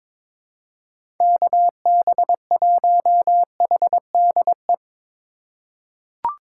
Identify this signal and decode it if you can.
Morse code, heard as a keyed tone, about 22 wpm, 700 Hz. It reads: KB1HDE
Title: USA Call Signs - 4x Character Spacing - 22wpm